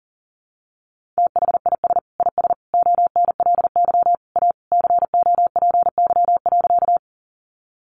Morse code, heard as a keyed tone, about 40 wpm, 700 Hz. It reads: THIS IS ONLY A COPY.
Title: Sentences from Top 1400 Words 40wpm